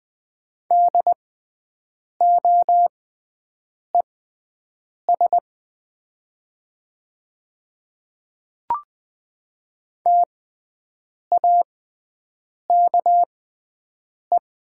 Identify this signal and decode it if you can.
Morse code, heard as a keyed tone, about 20 wpm, spaced as for 7 wpm, 700 Hz. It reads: DOES TAKE
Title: 6x Character Spacing - Top 200 Words - 20wpm